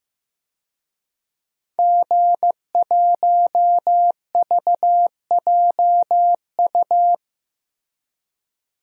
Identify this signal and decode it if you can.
Morse code, heard as a keyed tone, about 15 wpm, 700 Hz. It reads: G1VJU
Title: DXCC-100 Call Signs - 5 Characters - Repeated 3x - 15wpm